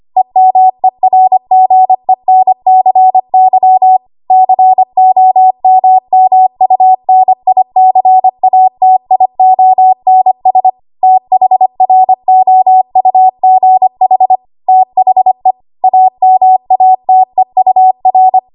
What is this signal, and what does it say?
Morse code, heard as a keyed tone, about 25 wpm, 750 Hz. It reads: EMERGENCY COMMUNICATIONS THROUGH THE AMATEUR
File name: Archive / 25 WPM / 25WPM